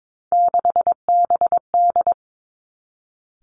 Morse code, heard as a keyed tone, about 22 wpm, 700 Hz. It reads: 6BD
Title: Three Challenging Letter-Number Combinations 22wpm